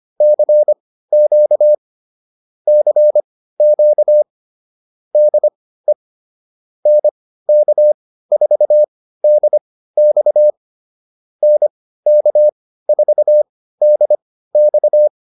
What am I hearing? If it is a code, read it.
Morse code, heard as a keyed tone, about 25 wpm, spaced as for 15 wpm, 600 Hz. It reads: CQ CQ DE NK4DX NK4DX